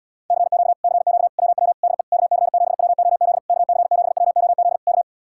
Morse code, heard as a keyed tone, about 77 wpm, 700 Hz. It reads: CQ CQ WW DE KC7WOZ KC7WOZ K